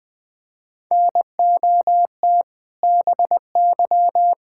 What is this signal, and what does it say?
Morse code, heard as a keyed tone, about 21 wpm, 700 Hz. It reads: NOT BY
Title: Sentences from Top 1000 Words - Review Part 5 - 20wpm